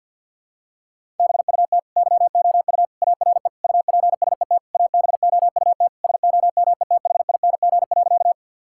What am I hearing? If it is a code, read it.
Morse code, heard as a keyed tone, about 50 wpm, 700 Hz. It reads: BUT YOU ARE UPSET ABOUT SOMETHING.
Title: Sentences from Top 1000 Words - Review Part 5 - 50wpm